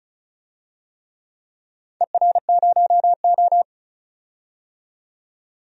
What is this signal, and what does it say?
Morse code, heard as a keyed tone, about 36 wpm, 700 Hz. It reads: ER0O
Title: DXCC-100 Call Signs - 4 Characters - 35wpm